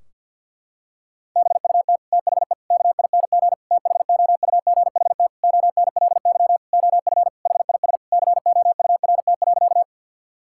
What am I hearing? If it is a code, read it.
Morse code, heard as a keyed tone, about 51 wpm, 700 Hz. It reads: BUT THE KING THOUGHT ONLY OF HIS COURT.
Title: Sentences from Top 1600 Words 50wpm